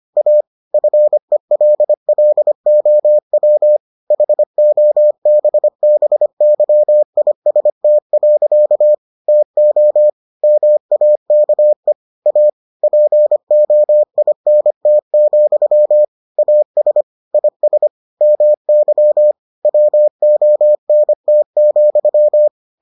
Morse code, heard as a keyed tone, about 25 wpm, 600 Hz. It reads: A FELLOW HOBBYIST. TO MAKE A POINT, AS IS MY WONT,